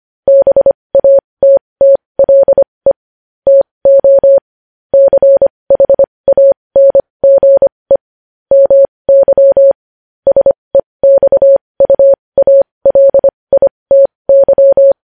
Morse code, heard as a keyed tone, about 25 wpm, spaced as for 20 wpm, 550 Hz. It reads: BATTLE TO CHANGE MY SEXUALITY
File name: podcast / News headlines-1x-25x20